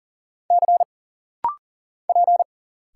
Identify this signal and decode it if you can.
Morse code, heard as a keyed tone, about 40 wpm, 700 Hz. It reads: C P
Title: Single Letters - Rapid-Fire 40wpm